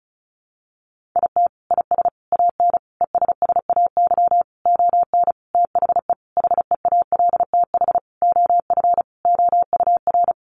Morse code, heard as a keyed tone, about 34 wpm, 700 Hz. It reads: IT IS AN ESSAY ON THE HEALTH OF OUR
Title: Sentences from Top 1500 Words 35wpm